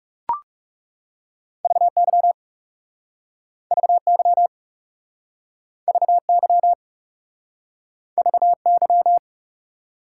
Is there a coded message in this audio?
Yes: Morse code from 1.6 to 9.2 s, heard as a keyed tone, about 42 wpm, 700 Hz. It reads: VY VY VY VY